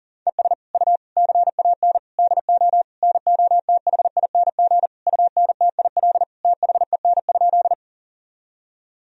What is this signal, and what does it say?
Morse code, heard as a keyed tone, about 40 wpm, 700 Hz. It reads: ES U CAN DO NOTHING UNTIL THEN?
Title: Sentences from Top 500 Words and Abbreviations 40wpm